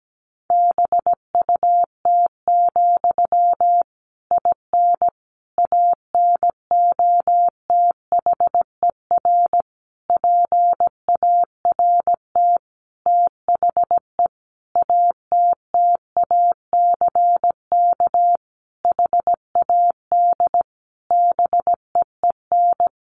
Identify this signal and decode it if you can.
Morse code, heard as a keyed tone, about 17 wpm, 700 Hz. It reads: BUT, IN ANOTHER PART THE ATTACK HAD BEEN
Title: Sentences from Top 1500 Words 17wpm